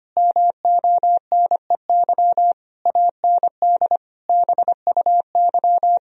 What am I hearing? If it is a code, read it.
Morse code, heard as a keyed tone, about 25 wpm, 700 Hz. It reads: MONEY AND BUY
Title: Sentences from Top 1000 Words - Review Part 4 - 25wpm